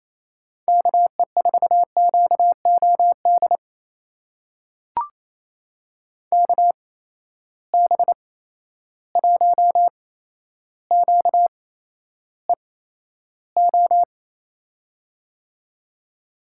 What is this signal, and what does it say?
Morse code, heard as a keyed tone, about 28 wpm, spaced as for 12 wpm, 700 Hz. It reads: KE4QOD KB1QEO
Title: USA Call Signs - 6 Characters - 8x Character Spacing - 28wpm